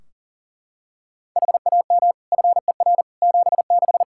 Morse code, heard as a keyed tone, about 40 wpm, 700 Hz. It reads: HAM FER 76
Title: Practice QSO - Exchange 3 - More Info - 40wpm